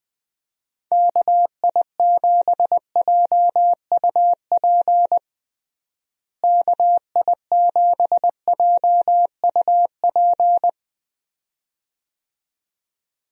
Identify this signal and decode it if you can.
Morse code, heard as a keyed tone, about 21 wpm, 700 Hz. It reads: KI7JUP KI7JUP
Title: USA Call Signs - 6 Characters - Repeated 2x - 20wpm